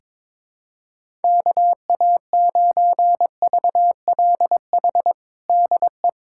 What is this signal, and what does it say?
Morse code, heard as a keyed tone, about 22 wpm, 700 Hz. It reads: KA9VLH DE
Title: USA Callsign DE Callsign - 6 Characters - 22wpm